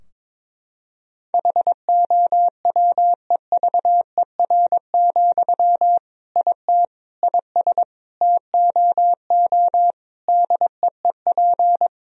Morse code, heard as a keyed tone, about 22 wpm, 700 Hz. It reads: HOWEVER, IT IS TOO DEEP
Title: Sentences from Top 1000 Words - Review Part 3 - 22wpm